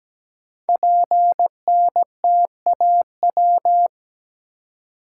Morse code, heard as a keyed tone, about 17 wpm, 700 Hz. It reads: PNTAW
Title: Code Groups - 5 Letters - 17wpm